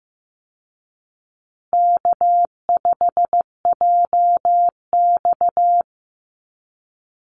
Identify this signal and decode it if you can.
Morse code, heard as a keyed tone, about 15 wpm, 700 Hz. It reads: K5JX
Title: Call Signs Repeated 2x - 15wpm